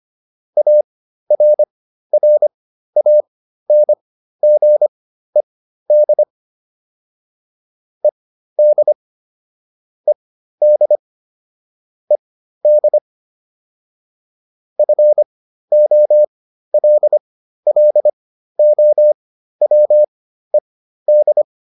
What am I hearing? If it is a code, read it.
Morse code, heard as a keyed tone, about 25 wpm, spaced as for 13 wpm, 600 Hz. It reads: ARRANGED ED ED ED FOLLOWED